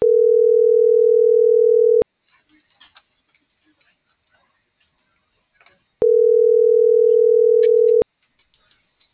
Ringback tone (North America), 440 + 480 Hz.